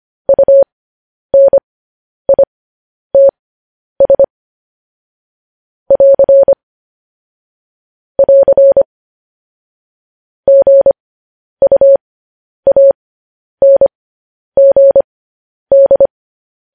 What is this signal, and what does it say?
Morse code, heard as a keyed tone, about 25 wpm, spaced as for 10 wpm, 550 Hz. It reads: UNITS <AR> <AR> GUANGD